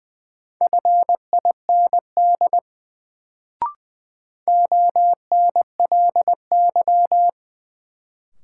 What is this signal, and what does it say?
Morse code, heard as a keyed tone, about 20 wpm, 700 Hz. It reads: FIND ONLY
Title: Sets of 2 Words - New Word Drill - 20wpm